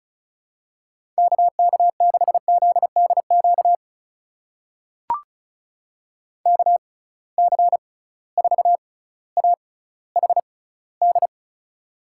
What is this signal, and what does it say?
Morse code, heard as a keyed tone, about 36 wpm, spaced as for 18 wpm, 700 Hz. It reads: KK6ZDQ KC4AHD
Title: USA Call Signs - 6 Characters - 6x Character Spacing - 35wpm